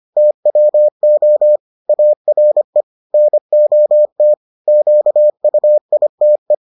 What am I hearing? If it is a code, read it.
Morse code, heard as a keyed tone, about 25 wpm, 600 Hz. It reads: TWO ARE NOT QUITE